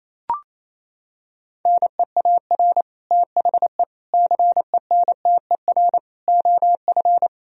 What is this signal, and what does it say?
Morse code, heard as a keyed tone, about 28 wpm, 700 Hz. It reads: NEAR THE CENTER OF
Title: Sentences from Top 1000 Words - Review Part 2 - 28wpm